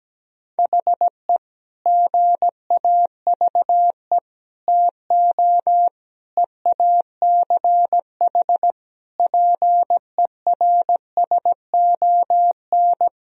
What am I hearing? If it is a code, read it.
Morse code, heard as a keyed tone, about 17 wpm, 700 Hz. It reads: HE GAVE TO EACH PERSON
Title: Sentences from Top 1000 Words - Review Part 4 - 17wpm